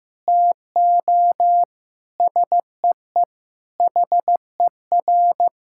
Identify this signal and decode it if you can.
Morse code, heard as a keyed tone, about 15 wpm, 700 Hz. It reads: TO SEE HER
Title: Sentences from Top 1000 Words - Review Part 6 - 15wpm